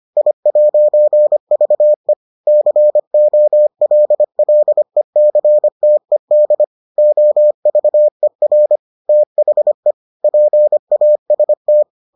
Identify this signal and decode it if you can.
Morse code, heard as a keyed tone, about 25 wpm, 600 Hz. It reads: I'VE COLLECTED OVER THE PAST